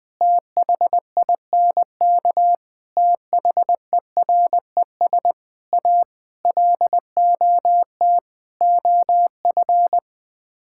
Morse code, heard as a keyed tone, about 20 wpm, 700 Hz. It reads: THINK THERES A LOT OF
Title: Sets of 6 Words - 20wpm